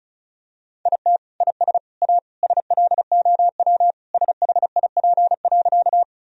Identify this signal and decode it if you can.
Morse code, heard as a keyed tone, about 36 wpm, 700 Hz. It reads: IT IS A SLOW SHIP.
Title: Sentences from Top 1000 Words - Review Part 2 - 35wpm